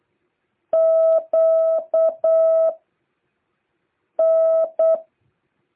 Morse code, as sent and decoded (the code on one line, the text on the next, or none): --.- / -.
Q N